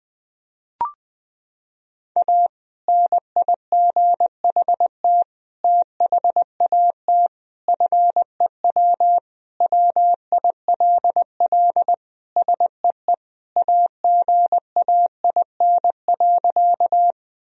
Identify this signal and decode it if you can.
Morse code, heard as a keyed tone, about 20 wpm, 700 Hz. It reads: A NIGHT THAT FEW WILL SEE AGAIN.
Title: Sentences from Top 1000 Words - Review Part 1 - 20wpm